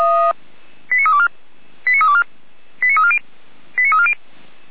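Five-tone selective call (ZVEI), heard as quick runs of five tones, tones of about 70 ms. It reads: 89324 89324 89340 89340
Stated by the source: ZVEI five-tone alert received by SDR.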